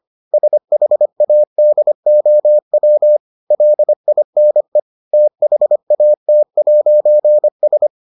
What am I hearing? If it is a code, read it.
Morse code, heard as a keyed tone, about 25 wpm, 600 Hz. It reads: SHADOW LINE THAT'S